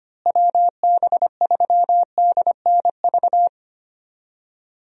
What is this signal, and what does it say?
Morse code, heard as a keyed tone, about 25 wpm, 700 Hz. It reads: WB3DNV
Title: USA Call Signs - 6 Characters - Repeated 2x - 25wpm